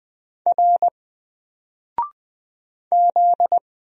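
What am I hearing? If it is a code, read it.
Morse code, heard as a keyed tone, about 20 wpm, 700 Hz. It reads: R Z
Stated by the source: Single letter and number Morse drill at 20 wpm, from N4LCD.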